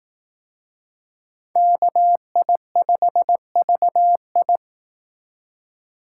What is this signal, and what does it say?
Morse code, heard as a keyed tone, about 18 wpm, 700 Hz. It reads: KI5VI
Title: Speed Racing - USA 2x2 Call Signs 15wpm